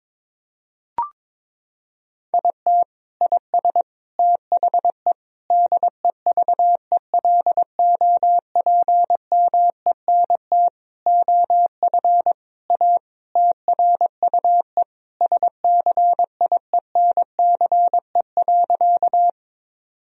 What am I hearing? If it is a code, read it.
Morse code, heard as a keyed tone, about 22 wpm, 700 Hz. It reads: IT IS THE DEVELOPMENT OF A TRUE SCIENCE.
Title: Sentences from Top 1000 Words - Review Part 4 - 22wpm